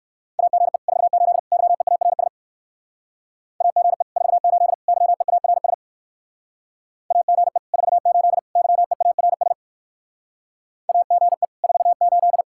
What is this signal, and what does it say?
Morse code, heard as a keyed tone, about 59 wpm, 700 Hz. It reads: AGE 48 YEARS AGE 48 YEARS AGE 48 YEARS AGE 48